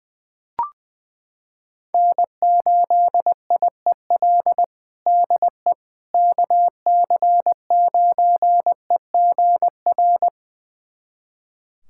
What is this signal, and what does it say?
Morse code, heard as a keyed tone, about 21 wpm, 700 Hz. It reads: N8IEL DE KC9EGR
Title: USA Callsign DE Callsign 20wpm